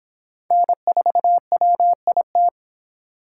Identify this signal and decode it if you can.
Morse code, heard as a keyed tone, about 24 wpm, 700 Hz. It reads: N4WIT